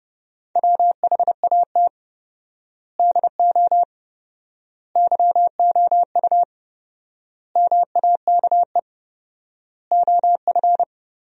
Morse code, heard as a keyed tone, about 30 wpm, 700 Hz. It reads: WHAT DO YOU MAKE OF
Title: 4x Word-Spacing - Sentences from Top 100 Words - 30wpm